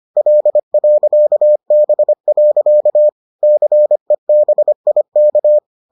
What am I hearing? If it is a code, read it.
Morse code, heard as a keyed tone, about 25 wpm, 600 Hz. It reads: L.B. CEBIK